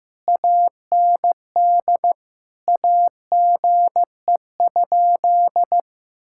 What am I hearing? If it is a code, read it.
Morse code, heard as a keyed tone, about 16 wpm, 700 Hz. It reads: AND AGE?
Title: Sentences from Top 1600 Words 15wpm